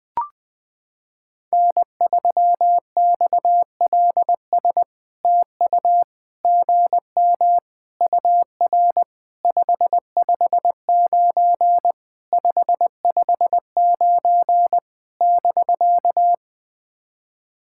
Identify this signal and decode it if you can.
Morse code, heard as a keyed tone, about 20 wpm, 700 Hz. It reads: N3XLS TU GM UR 559 559 <BK>
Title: POTA QSO Practice - 20wpm